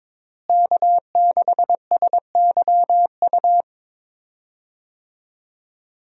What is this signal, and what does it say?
Morse code, heard as a keyed tone, about 22 wpm, 700 Hz. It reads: K6SYU
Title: USA Call Signs - 5 Characters - Repeated 2x - 22wpm